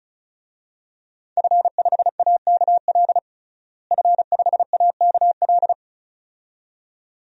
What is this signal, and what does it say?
Morse code, heard as a keyed tone, about 36 wpm, 700 Hz. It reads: F5AKL F5AKL
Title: DXCC-100 Call Signs - 5 Characters - Repeated 2x - 35wpm